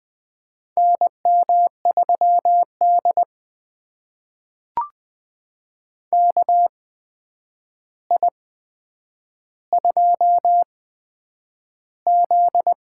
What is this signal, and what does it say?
Morse code, heard as a keyed tone, about 20 wpm, spaced as for 9 wpm, 700 Hz. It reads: NM3D KI2Z
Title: USA Call Signs - 4 Characters - 8x Character Spacing - 20wpm